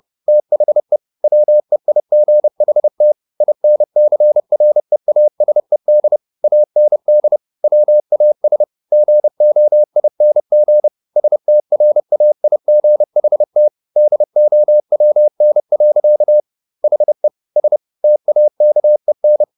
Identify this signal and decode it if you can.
Morse code, heard as a keyed tone, about 30 wpm, 600 Hz. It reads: THE WEIGHT INCREASED AND WAS GOING STRAIGHT DOWN. HE S TAKEN
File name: the old man and the sea / the old man and the sea 1 30wpm